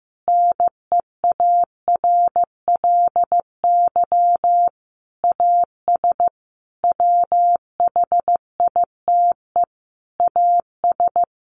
Morse code, heard as a keyed tone, about 15 wpm, 700 Hz. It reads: NEARLY AS WHITE AS
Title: Sentences from Top 1500 Words 15wpm